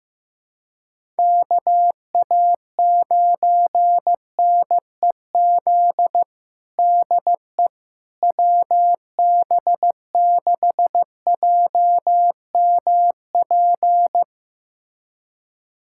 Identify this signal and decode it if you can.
Morse code, heard as a keyed tone, about 15 wpm, 700 Hz. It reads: KA9NEZ DE WB6JMP
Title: USA Callsign DE Callsign 15wpm